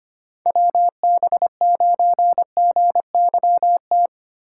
Morse code, heard as a keyed tone, about 25 wpm, 700 Hz. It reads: WB9GYT
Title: USA Call Signs - 6 Characters - 2x Character Spacing - 25wpm